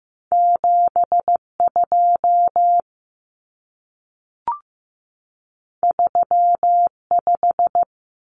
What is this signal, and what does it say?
Morse code, heard as a keyed tone, about 15 wpm, 700 Hz. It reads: 72 35
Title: Numbers - 2-Digit 15wpm